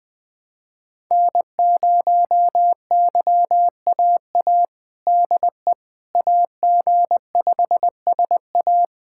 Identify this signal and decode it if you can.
Morse code, heard as a keyed tone, about 20 wpm, 700 Hz. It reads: N0YAA DE AG5SA
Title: USA Callsign DE Callsign 20wpm